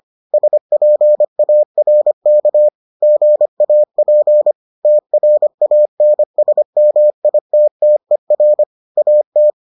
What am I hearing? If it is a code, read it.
Morse code, heard as a keyed tone, about 25 wpm, 600 Hz. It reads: SPARK GAP TRANSMITTER AT